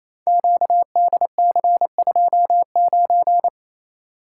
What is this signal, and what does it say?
Morse code, heard as a keyed tone, about 29 wpm, 700 Hz. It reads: QDC29
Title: Code Groups - 5 Letter:Numbers - 28wpm